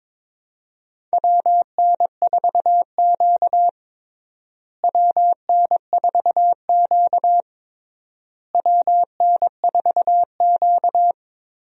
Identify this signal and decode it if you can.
Morse code, heard as a keyed tone, about 22 wpm, 700 Hz. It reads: WN4Q WN4Q WN4Q
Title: USA Call Signs - 4 Characters - Repeated 3x - 22wpm